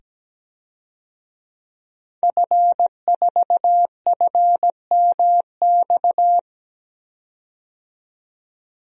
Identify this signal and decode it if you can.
Morse code, heard as a keyed tone, about 17 wpm, 700 Hz. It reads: F4FMX
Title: DXCC-100 Call Signs - 5 Characters - 17wpm